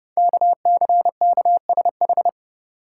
Morse code, heard as a keyed tone, about 31 wpm, 700 Hz. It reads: KCKSH
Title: Code Groups - 5 Letters - 30wpm